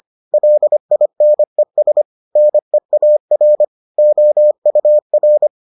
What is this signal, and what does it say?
Morse code, heard as a keyed tone, about 25 wpm, 600 Hz. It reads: LINES NEAR OUR